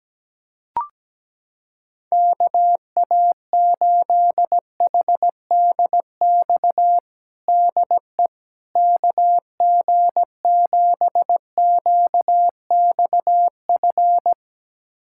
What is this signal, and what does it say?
Morse code, heard as a keyed tone, about 17 wpm, 700 Hz. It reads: KA8HDX DE KG7QXF
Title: USA Callsign DE Callsign - 6 Characters - 17wpm